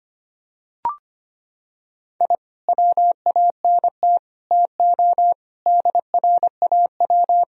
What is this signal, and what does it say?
Morse code, heard as a keyed tone, about 25 wpm, 700 Hz. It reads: I WANT TO DRAW
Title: Sentences from Top 1000 Words - Review Part 1 - 25wpm